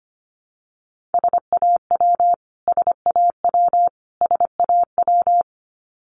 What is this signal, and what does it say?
Morse code, heard as a keyed tone, about 25 wpm, 700 Hz. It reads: SAW SAW SAW
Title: Top 300 Words Repeated 25wpm